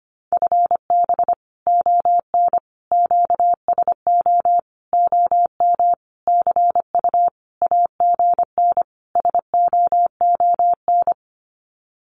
Morse code, heard as a keyed tone, about 25 wpm, 700 Hz. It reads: FB ON QSO OM CU AGN SOON
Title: Common QSO Phrases 25wpm